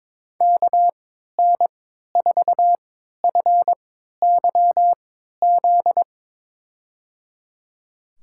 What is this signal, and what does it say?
Morse code, heard as a keyed tone, about 22 wpm, spaced as for 13 wpm, 700 Hz. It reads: KN4FYZ
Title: USA Call Signs - 6 Characters - 3x Character Spacing - 22wpm